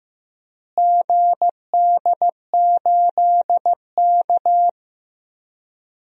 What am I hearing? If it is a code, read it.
Morse code, heard as a keyed tone, about 15 wpm, 700 Hz. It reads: GD8K
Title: DXCC-100 Call Signs - 4 Characters - Repeated 3x - 15wpm